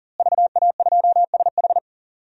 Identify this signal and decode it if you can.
Morse code, heard as a keyed tone, about 38 wpm, 700 Hz. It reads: VA2SH
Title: DXCC-100 Call Signs - 5 Characters - Repeated 3x - 40wpm